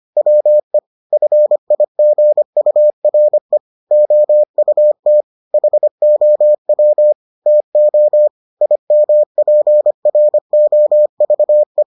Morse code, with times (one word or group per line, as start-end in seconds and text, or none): 0.2-0.8 WE
1.1-3.6 FIGURE
3.9-5.2 OUT
5.5-7.1 HOW
7.5-8.3 TO
8.6-11.8 IMPROVE